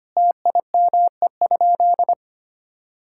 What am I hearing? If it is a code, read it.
Morse code, heard as a keyed tone, about 25 wpm, 700 Hz. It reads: TIME?
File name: practice / 25wpm / Sentences with 3-Word Phrases - 25wpm